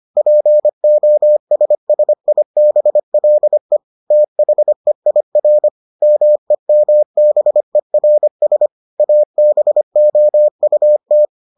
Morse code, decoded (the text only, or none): POSSIBLE THEIR MEMBERS ABOUT